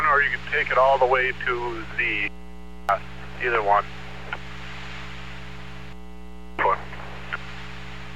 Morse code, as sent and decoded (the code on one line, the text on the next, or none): none